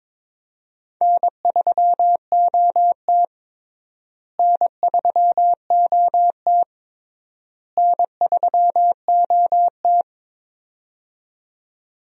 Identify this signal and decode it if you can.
Morse code, heard as a keyed tone, about 22 wpm, 700 Hz. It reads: N3OT N3OT N3OT
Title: USA Call Signs - 4 Characters - Repeated 3x - 22wpm